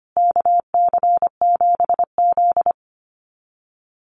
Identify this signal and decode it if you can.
Morse code, heard as a keyed tone, about 25 wpm, 700 Hz. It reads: KC7Z
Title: USA Call Signs Repeated 3x 25wpm